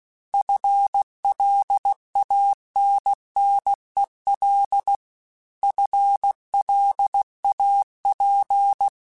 Morse code, heard as a keyed tone, about 16 wpm, 800 Hz. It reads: FLANNEL FLAP
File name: FL Runs / FL Runs mp3s / FL Runs @15